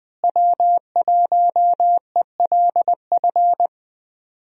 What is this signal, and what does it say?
Morse code, heard as a keyed tone, about 20 wpm, 700 Hz. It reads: W1ELF
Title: USA Call Signs - 5 Characters - Repeated 2x - 20wpm